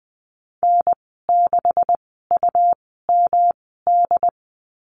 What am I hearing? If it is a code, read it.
Morse code, heard as a keyed tone, about 20 wpm, spaced as for 14 wpm, 700 Hz. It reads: N6UMD
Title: 167 Lesson 27g - Number 7 - Call Signs 20wpm